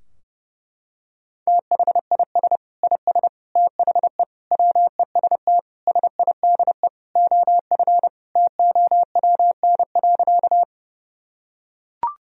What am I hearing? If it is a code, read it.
Morse code, heard as a keyed tone, about 30 wpm, 700 Hz. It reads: THIS IS THE WEST SIDE OF TOWN.
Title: Sentences from Top 1000 Words - Review Part 2 - 30wpm